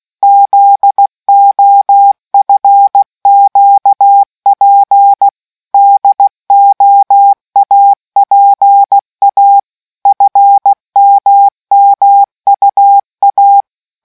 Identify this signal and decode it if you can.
Morse code, heard as a keyed tone, about 16 wpm, 800 Hz. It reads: ZOFQP DOAPA FMMUA